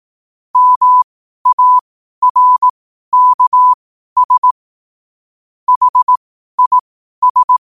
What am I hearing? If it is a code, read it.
Morse code, heard as a keyed tone, about 18 wpm, spaced as for 12 wpm, 1000 Hz. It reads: MARKS HIS
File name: qotdfiles / 12wpm